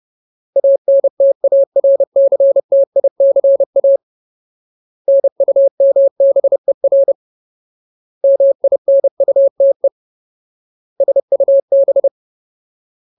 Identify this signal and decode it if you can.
Morse code, heard as a keyed tone, about 30 wpm, 550 Hz. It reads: ANTARCTICA NUMBER MINUTE SUB
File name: Ultimate Word List / 30WPM Ultimate